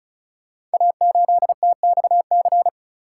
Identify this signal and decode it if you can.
Morse code, heard as a keyed tone, about 36 wpm, 690 Hz. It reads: A8TXC
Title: Code Groups - 5 Letter:Numbers - 35wpm